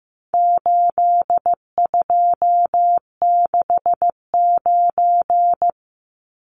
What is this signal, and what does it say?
Morse code, heard as a keyed tone, about 15 wpm, 700 Hz. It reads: 8269
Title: Numbers - 4-Digit 15wpm